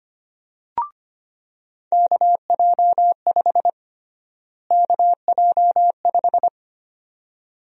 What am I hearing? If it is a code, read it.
Morse code, heard as a keyed tone, about 25 wpm, 700 Hz. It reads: KJ5 KJ5
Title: USA Call Sign Prefixes - Repeated 2x - 25wpm